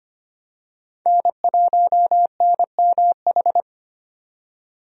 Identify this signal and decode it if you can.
Morse code, heard as a keyed tone, about 25 wpm, 700 Hz. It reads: N1NMH
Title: USA Call Signs - 5 Characters - 25wpm